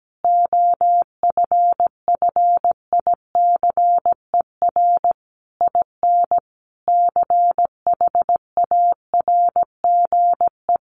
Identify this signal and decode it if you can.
Morse code, heard as a keyed tone, about 17 wpm, 700 Hz. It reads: OFFICER IN CHARGE
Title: Sentences from Top 1200 Words 17wpm